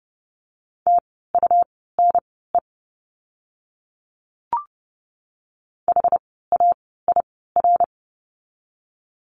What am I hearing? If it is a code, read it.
Morse code, heard as a keyed tone, about 30 wpm, spaced as for 17 wpm, 700 Hz. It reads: TUNE HAIR
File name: Lesson 17f - Letter u - Words 30wpm